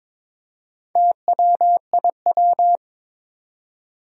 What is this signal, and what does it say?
Morse code, heard as a keyed tone, about 22 wpm, 700 Hz. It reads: TWIW
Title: Four Challenging Letter-Number Combinations 22wpm